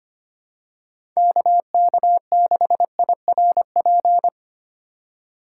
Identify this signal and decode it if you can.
Morse code, heard as a keyed tone, about 25 wpm, 700 Hz. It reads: KK6IRP
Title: USA Call Signs - 6 Characters - 2x Character Spacing - 25wpm